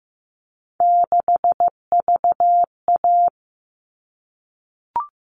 Morse code, heard as a keyed tone, about 15 wpm, 700 Hz. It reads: 6VA
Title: All Cut Numbers - 3-Digit 15wpm